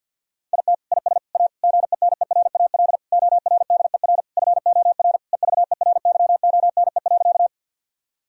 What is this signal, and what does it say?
Morse code, heard as a keyed tone, about 49 wpm, 700 Hz. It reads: IT IS A GENERAL ORDER FOR EVERYONE.
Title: Sentences from Top 1000 Words - Review Part 3 - 50wpm